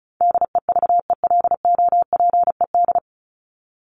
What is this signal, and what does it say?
Morse code, heard as a keyed tone, about 34 wpm, 700 Hz. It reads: DEVELOPED
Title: ED Suffix - Top 100 Words with Base Words 35wpm